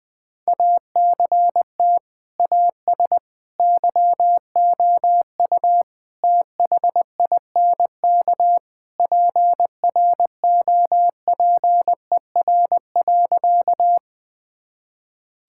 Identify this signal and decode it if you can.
Morse code, heard as a keyed tone, about 21 wpm, 700 Hz. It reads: ACT AS YOU THINK PROPER.
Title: Sentences from Top 1000 Words - Review Part 5 - 20wpm